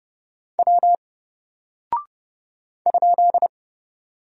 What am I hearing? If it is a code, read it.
Morse code, heard as a keyed tone, about 30 wpm, 700 Hz. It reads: W ?
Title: Single Character (Letter:Number:Punctuation:Prosign) Alt 30wpm